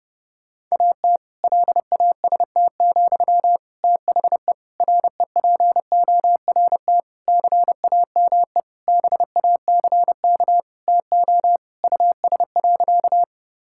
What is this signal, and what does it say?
Morse code, heard as a keyed tone, about 30 wpm, 700 Hz. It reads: AT LAST, THE REPORT CAME BACK TO US.
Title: Sentences from Top 1000 Words - Review Part 6 - 30wpm